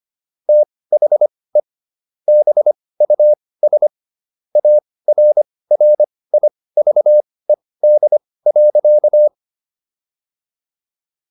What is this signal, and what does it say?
Morse code, heard as a keyed tone, about 25 wpm, spaced as for 18 wpm, 600 Hz. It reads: THE BUS ARRIVED.